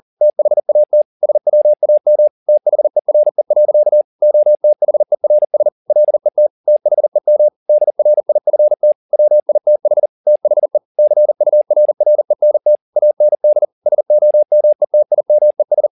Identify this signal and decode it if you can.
Morse code, heard as a keyed tone, about 40 wpm, 600 Hz. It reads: THAT SWAM THERE. OTHERS LET THEM DRIFT WITH THE CURRENT AND SOMETIMES